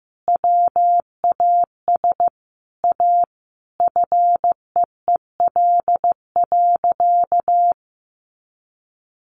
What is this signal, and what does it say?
Morse code, heard as a keyed tone, about 15 wpm, 700 Hz. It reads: WAS A FEEL.
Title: A Sound Of Thunder 15wpm